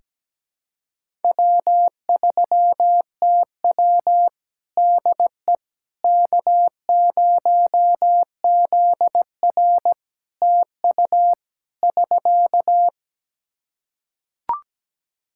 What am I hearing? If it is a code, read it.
Morse code, heard as a keyed tone, about 17 wpm, 700 Hz. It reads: W3TW DE K0ZR TU <SK>
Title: K1USN SST QSO Practice 17wpm